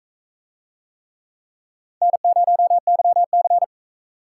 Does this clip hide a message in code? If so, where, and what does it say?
Yes: Morse code from 2.0 to 3.6 s, heard as a keyed tone, about 39 wpm, 700 Hz. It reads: N0YC